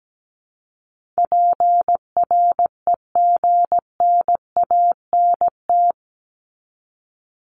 Morse code, heard as a keyed tone, about 17 wpm, 700 Hz. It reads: PREGNANT
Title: Top 751-1000 Words 17wpm